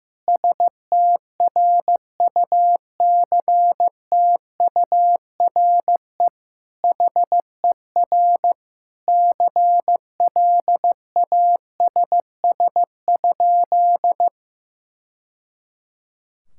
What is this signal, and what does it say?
Morse code, heard as a keyed tone, about 15 wpm, 700 Hz. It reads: STRUCTURE HER CLASS?
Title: Sentences from Top 1000 Words - Review Part 6 - 15wpm